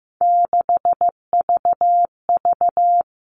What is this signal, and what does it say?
Morse code, heard as a keyed tone, about 15 wpm, 700 Hz. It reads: 6VV
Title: All Cut Numbers - 3-Digit 15wpm